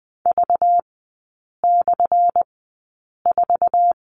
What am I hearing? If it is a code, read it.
Morse code, heard as a keyed tone, about 20 wpm, spaced as for 8 wpm, 690 Hz. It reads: V/4
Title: 191 Lesson 31e - Letter v - Three Characters 20wpm